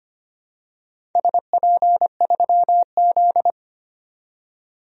Morse code, heard as a keyed tone, about 25 wpm, 700 Hz. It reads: SP3Z
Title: DXCC-100 Call Signs - 4 Characters - 25wpm